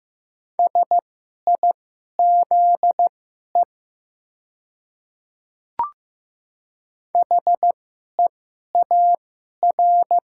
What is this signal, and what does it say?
Morse code, heard as a keyed tone, about 15 wpm, spaced as for 11 wpm, 700 Hz. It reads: SIZE HEAR
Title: 2x Character Spacing - Top 500 Words - 15wpm